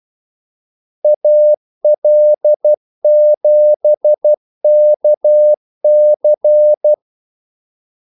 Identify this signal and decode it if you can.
Morse code, heard as a keyed tone, about 12 wpm, 600 Hz. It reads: AL7KC
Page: 2 ekstra ordmellemrum mellem kaldesignalerne